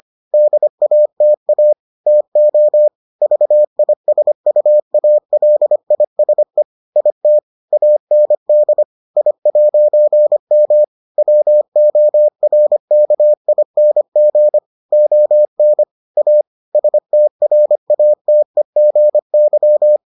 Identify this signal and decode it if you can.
Morse code, heard as a keyed tone, about 25 wpm, 600 Hz. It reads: DATA TO VISUALISE IT AND I'M WORKING ON A STRATEGY